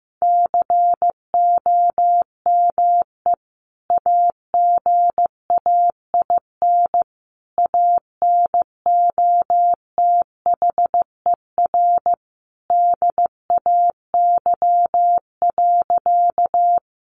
Morse code, heard as a keyed tone, about 14 wpm, 700 Hz. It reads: COME AGAIN ANOTHER DAY.
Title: Sentences from Top 600 Words 15wpm